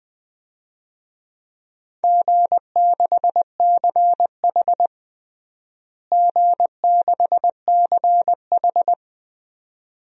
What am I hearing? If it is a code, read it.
Morse code, heard as a keyed tone, about 20 wpm, 700 Hz. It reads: G6CH G6CH
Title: DXCC-100 Call Signs - 4 Characters - Repeated 2x - 20wpm